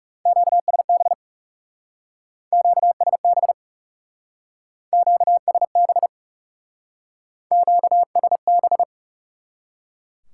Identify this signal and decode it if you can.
Morse code, heard as a keyed tone, about 37 wpm, 700 Hz. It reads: QSB QSB QSB QSB
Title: Speed-Racing Top 90 QSO Elements 30wpm